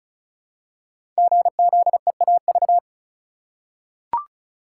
Morse code, heard as a keyed tone, about 35 wpm, 700 Hz. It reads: GZEAV
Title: Code Groups - 5 Letter:Numbers - 35wpm